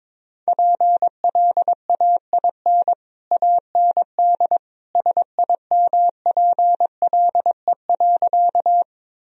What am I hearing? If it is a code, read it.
Morse code, heard as a keyed tone, about 22 wpm, 700 Hz. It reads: PLAIN AND SIMPLE.
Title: Sentences from Top 1000 Words - Review Part 2 - 22wpm